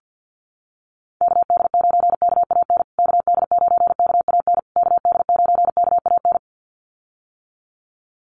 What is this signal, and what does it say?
Morse code, heard as a keyed tone, about 49 wpm, 700 Hz. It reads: KD9KAN KD9KAN KD9KAN
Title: USA Call Signs Repeated 3x 50wpm